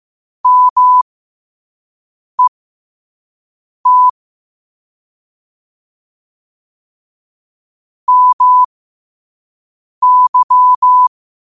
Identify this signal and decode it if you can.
Morse code, heard as a keyed tone, about 15 wpm, spaced as for 5 wpm, 1000 Hz. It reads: MET MY
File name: qotdfiles / 5wpm